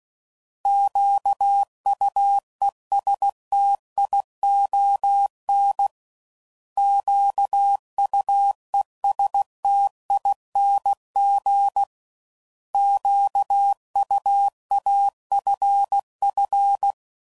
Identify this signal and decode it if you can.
Morse code, heard as a keyed tone, about 16 wpm, 800 Hz. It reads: QUESTION QUESTING QUAFF
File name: Prefixes / By First Letter / Qs